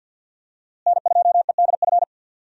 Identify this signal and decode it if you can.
Morse code, heard as a keyed tone, about 53 wpm, 700 Hz. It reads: N2EDF
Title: USA Call Signs - 5 Characters - Repeated 3x - 50wpm